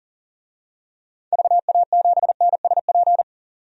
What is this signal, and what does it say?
Morse code, heard as a keyed tone, about 40 wpm, 700 Hz. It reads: VA7NSP
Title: DXCC-100 Call Signs - 6 Characters - 40wpm